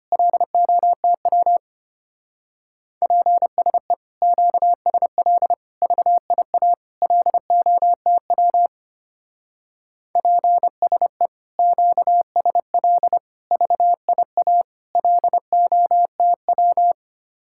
Morse code, heard as a keyed tone, about 32 wpm, 700 Hz. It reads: LOTW PSE QSL VIA LOTW PSE QSL VIA LOTW